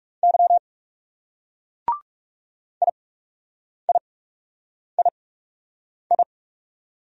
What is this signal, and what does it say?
Morse code, heard as a keyed tone, about 38 wpm, spaced as for 8 wpm, 700 Hz. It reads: Y IIII